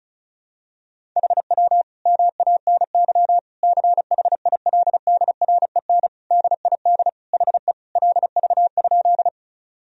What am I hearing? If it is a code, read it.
Morse code, heard as a keyed tone, about 36 wpm, 700 Hz. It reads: HW MANY CHILDREN DID HE LV?
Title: Sentences from Top 400 Words and Abbreviations 35wpm